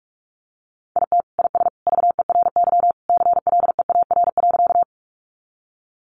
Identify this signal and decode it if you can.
Morse code, heard as a keyed tone, about 44 wpm, 700 Hz. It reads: IT IS VERY CLEAR.
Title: Sentences from Top 500 Words 45wpm